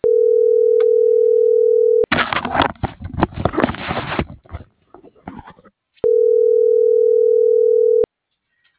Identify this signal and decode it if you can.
Ringback tone (North America), 440 + 480 Hz.